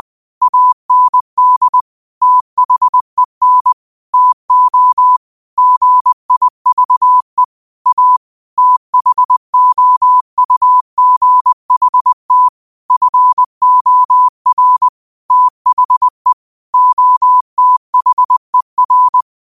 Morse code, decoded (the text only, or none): AND THEN TO GIVE A THOUGHT FOR THE OTHER